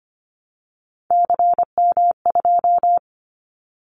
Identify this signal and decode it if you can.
Morse code, heard as a keyed tone, about 25 wpm, 700 Hz. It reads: CM2
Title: Callsign Prefixes for Top 100 DXCC - 25wpm